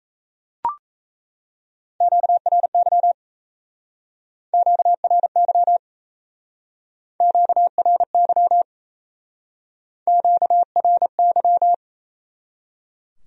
Morse code, heard as a keyed tone, about 38 wpm, 700 Hz. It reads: QRY QRY QRY QRY